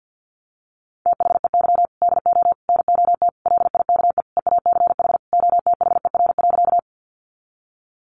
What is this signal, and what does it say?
Morse code, heard as a keyed tone, about 50 wpm, 700 Hz. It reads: THEY DO NOT LIKE EACH OTHER.
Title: Easy Phrases 50wpm